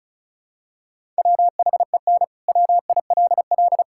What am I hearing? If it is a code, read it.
Morse code, heard as a keyed tone, about 34 wpm, 700 Hz. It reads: WHEN WILL
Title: Sentences from Top 1000 Words - Review Part 6 - 35wpm